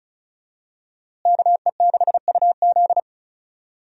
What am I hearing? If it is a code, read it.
Morse code, heard as a keyed tone, about 36 wpm, 700 Hz. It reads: KE6UZ
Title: USA Call Signs - 5 Characters - Repeated 3x - 35wpm